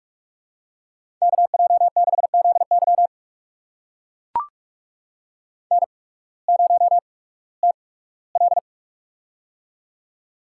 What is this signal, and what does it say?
Morse code, heard as a keyed tone, about 44 wpm, spaced as for 21 wpm, 700 Hz. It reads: KJ6ZY N0TL
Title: USA Call Signs - 8x Character Spacing - 45wpm